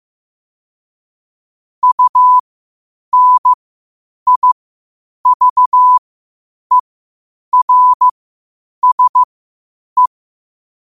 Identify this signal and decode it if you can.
Morse code, heard as a keyed tone, about 15 wpm, spaced as for 8 wpm, 1000 Hz. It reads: UNIVERSE
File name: qotdfiles / 8wpm